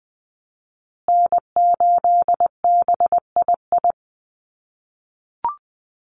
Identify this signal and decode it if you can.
Morse code, heard as a keyed tone, about 20 wpm, 700 Hz. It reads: N8BII
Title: USA Call Signs Repeated 2x 20wpm